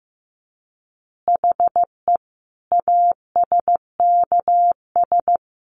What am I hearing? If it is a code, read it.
Morse code, heard as a keyed tone, about 15 wpm, 700 Hz. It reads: HE ASKS
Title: Easy Phrases 15wpm